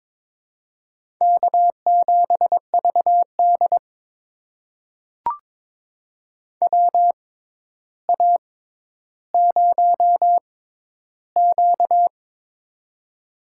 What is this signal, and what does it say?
Morse code, heard as a keyed tone, about 22 wpm, spaced as for 11 wpm, 700 Hz. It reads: K7VD WA0Q
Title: USA Call Signs - 4 Characters - 6x Character Spacing - 22wpm